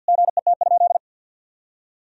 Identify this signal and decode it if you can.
Morse code, heard as a keyed tone, about 51 wpm, 700 Hz. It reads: GET?
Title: Sentences from Top 1000 Words - Review Part 3 - 50wpm